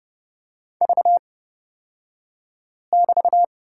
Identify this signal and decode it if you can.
Morse code, heard as a keyed tone, about 31 wpm, spaced as for 5 wpm, 700 Hz. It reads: V<BT>
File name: Lesson 37d - Letter x - Two Characters 30wpm